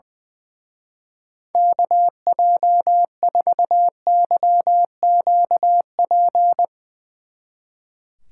Morse code, heard as a keyed tone, about 20 wpm, 700 Hz. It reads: KJ4YQP
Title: Speed Racing - USA 2x3 Call Signs 20wpm